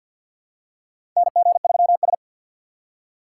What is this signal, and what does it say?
Morse code, heard as a keyed tone, about 50 wpm, 700 Hz. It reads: NG3S
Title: USA Call Signs - 4 Characters - 2x Character Spacing - 50wpm